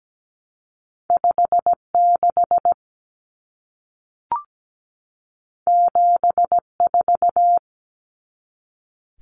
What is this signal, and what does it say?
Morse code, heard as a keyed tone, about 17 wpm, 700 Hz. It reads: 56 74
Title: Common Cut Numbers - 2-Digit 17wpm